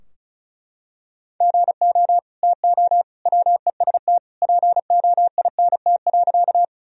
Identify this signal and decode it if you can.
Morse code, heard as a keyed tone, about 34 wpm, 700 Hz. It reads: GO TO WEST POINT.
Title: Sentences from Top 1000 Words - Review Part 2 - 35wpm